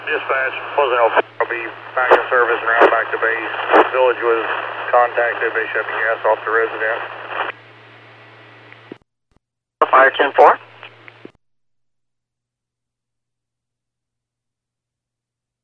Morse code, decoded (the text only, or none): none